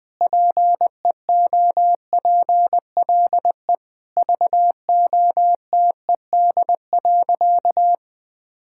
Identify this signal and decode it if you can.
Morse code, heard as a keyed tone, about 20 wpm, 700 Hz. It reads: PEOPLE VOTED.
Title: Sentences with 3-Word Phrases - 20wpm